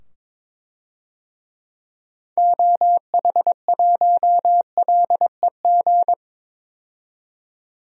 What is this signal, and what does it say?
Morse code, heard as a keyed tone, about 21 wpm, 700 Hz. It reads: OH1LEG
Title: DXCC-100 Call Signs - 6 Characters - 22wpm